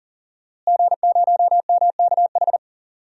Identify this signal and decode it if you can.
Morse code, heard as a keyed tone, about 38 wpm, 700 Hz. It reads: G0MKH
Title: DXCC-100 Call Signs - 5 Characters - Repeated 2x - 40wpm